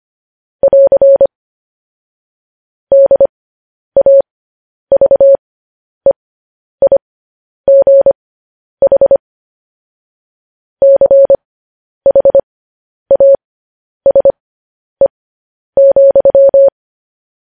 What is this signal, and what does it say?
Morse code, heard as a keyed tone, about 25 wpm, spaced as for 10 wpm, 550 Hz. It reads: <AR> DAVEIGH CHASE,
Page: news-headlines-in-morse-code-at-10-wpm